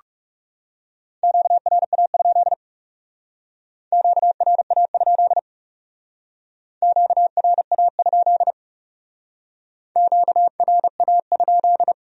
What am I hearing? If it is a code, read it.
Morse code, heard as a keyed tone, about 37 wpm, 700 Hz. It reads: QRA? QRA? QRA? QRA?
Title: Speed-Racing Q-Codes - 30wpm